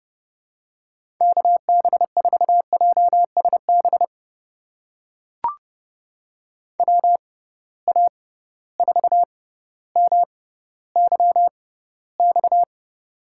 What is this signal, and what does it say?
Morse code, heard as a keyed tone, about 30 wpm, spaced as for 15 wpm, 700 Hz. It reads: KB4JSB WA4MYX
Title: USA Call Signs - 6 Characters - 6x Character Spacing - 30wpm